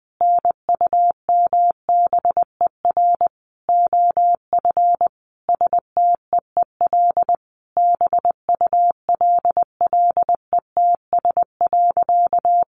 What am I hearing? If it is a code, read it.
Morse code, heard as a keyed tone, about 20 wpm, 700 Hz. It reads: NUMBER OF STEEL BULLETS.
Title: A Sound Of Thunder 20wpm